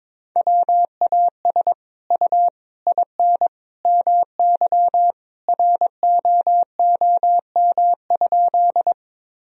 Morse code, heard as a keyed tone, about 22 wpm, 700 Hz. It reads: WAS U IN MY ROOM?
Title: Sentences from Top 300 Words and Abbreviations 22wpm